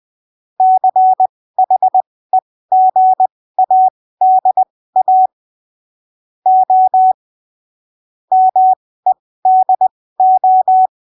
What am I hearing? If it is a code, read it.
Morse code, heard as a keyed tone, about 20 wpm, spaced as for 15 wpm, 750 Hz. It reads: CHEGADA O MEDO
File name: arquivos / texto / texto 19 20ppm